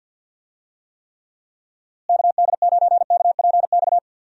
Morse code, heard as a keyed tone, about 50 wpm, 700 Hz. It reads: KD9KPX
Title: USA Call Signs - 6 Characters - Repeated 3x - 50wpm